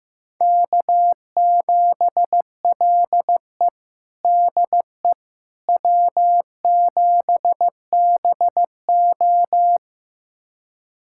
Morse code, heard as a keyed tone, about 15 wpm, 700 Hz. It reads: K7LE DE W7BO
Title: USA Callsign DE Callsign - 4 Characters - 15wpm